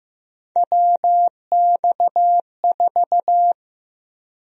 Morse code, heard as a keyed tone, about 15 wpm, 700 Hz. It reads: WX4
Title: USA Call Sign Prefixes - 15wpm